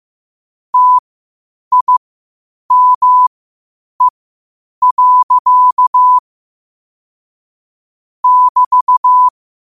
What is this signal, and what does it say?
Morse code, heard as a keyed tone, about 15 wpm, spaced as for 8 wpm, 1000 Hz. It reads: TIME. <BT>
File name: qotdfiles / 8wpm